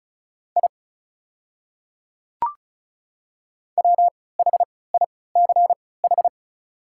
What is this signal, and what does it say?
Morse code, heard as a keyed tone, about 35 wpm, spaced as for 20 wpm, 690 Hz. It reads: I WHICH
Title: 3x Character Spacing - Top 100 Words - 35wpm